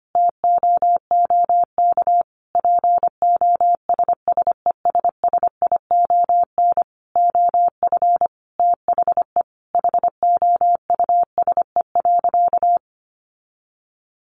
Morse code, heard as a keyed tone, about 25 wpm, 700 Hz. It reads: TOOK POSSESSION OF THE HOUSE.
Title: Sentences from Top 1400 Words 25wpm